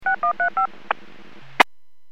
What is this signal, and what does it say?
DTMF, heard as telephone touch tones, tones of about 110 ms. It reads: 6135
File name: WPS / 00179 09-09 WPS 451.25 WPS